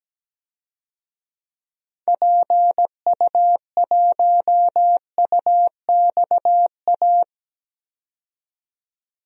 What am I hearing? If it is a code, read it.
Morse code, heard as a keyed tone, about 17 wpm, 700 Hz. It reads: PU1UXA
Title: DXCC-100 Call Signs - 6 Characters - 17wpm